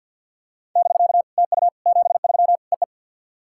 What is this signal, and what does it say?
Morse code, heard as a keyed tone, about 51 wpm, 690 Hz. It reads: <BK> TU 73 EE
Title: SOTA QSO Practice 50wpm